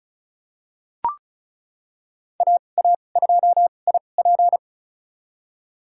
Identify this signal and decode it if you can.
Morse code, heard as a keyed tone, about 34 wpm, spaced as for 25 wpm, 700 Hz. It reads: AA2IP
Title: USA Call Signs - 5 Characters - 2x Character Spacing - 35wpm